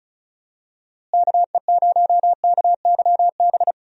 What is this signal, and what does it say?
Morse code, heard as a keyed tone, about 34 wpm, 700 Hz. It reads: KE0KYB
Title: USA Call Signs - 6 Characters - Repeated 3x - 35wpm